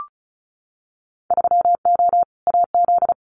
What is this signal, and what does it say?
Morse code, heard as a keyed tone, about 34 wpm, 700 Hz. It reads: 3O AZ
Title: Winter Field Day Exchanges 35wpm